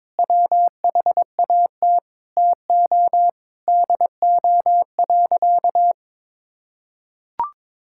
Morse code, heard as a keyed tone, about 22 wpm, 700 Hz. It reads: WHAT TO DO.
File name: practice / 22wpm / Sentences with 4-Word Phrases - 22wpm